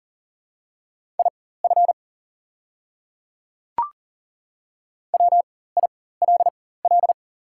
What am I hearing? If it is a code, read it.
Morse code, heard as a keyed tone, about 40 wpm, spaced as for 19 wpm, 700 Hz. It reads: IF WILL
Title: 4x Character Spacing - Top 100 Words - 40wpm